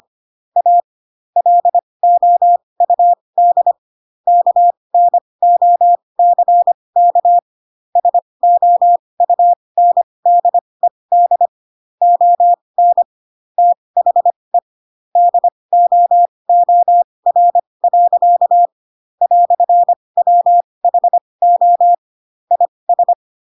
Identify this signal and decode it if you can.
Morse code, heard as a keyed tone, about 25 wpm, spaced as for 20 wpm, 700 Hz. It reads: A LOUD KNOCK SOUNDED ON THE DOOR. "WHO IS
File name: Pinocchio / Pinocchio ch2 15 25 ik1hge